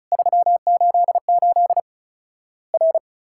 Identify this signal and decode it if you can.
Morse code, heard as a keyed tone, about 34 wpm, 700 Hz. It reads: 388 R
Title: Worked All Europe DX Contest - QTC Exchange - 35wpm